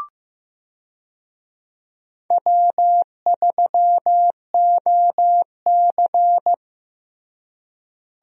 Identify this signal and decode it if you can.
Morse code, heard as a keyed tone, about 16 wpm, 700 Hz. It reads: W3OC
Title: USA Call Signs - 4 Characters - Repeated 2x - 15wpm